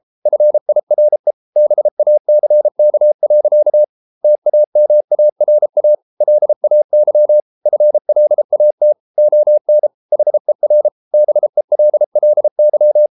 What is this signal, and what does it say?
Morse code, heard as a keyed tone, about 33 wpm, 600 Hz. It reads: FIRE BACK. TAMARA LAY FLAT ON HER BELLY